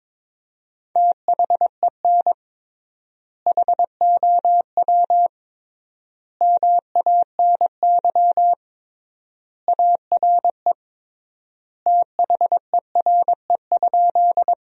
Morse code, heard as a keyed tone, about 22 wpm, 700 Hz. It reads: THEN HOW MANY ARE THERE?
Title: 3x Word-Spacing - Sentences from Top 100 Words - 22wpm